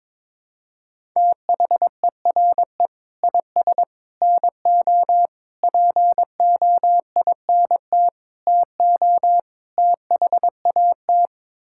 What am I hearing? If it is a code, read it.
Morse code, heard as a keyed tone, about 22 wpm, 700 Hz. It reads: THERE IS NO POINT TO THAT
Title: Sentences from Top 1000 Words - Review Part 5 - 22wpm